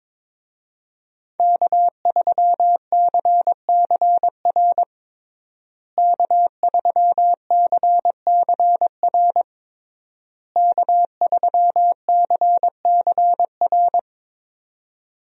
Morse code, heard as a keyed tone, about 22 wpm, 700 Hz. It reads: K3CCR K3CCR K3CCR
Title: USA Call Signs - 5 Characters - Repeated 3x - 22wpm